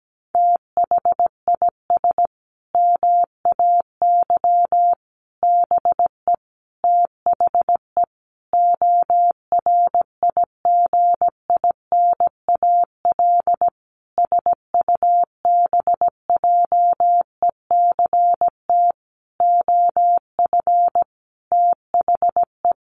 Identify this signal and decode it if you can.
Morse code, heard as a keyed tone, about 17 wpm, 700 Hz. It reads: THIS MAY BE THE ORIGINAL SUBJECT OF THE
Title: Sentences from Top 900 Words 17wpm